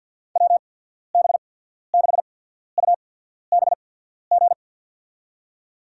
Morse code, heard as a keyed tone, about 48 wpm, spaced as for 14 wpm, 690 Hz. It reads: WB6UBG
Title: USA Call Signs - 6 Characters - 8x Character Spacing - 50wpm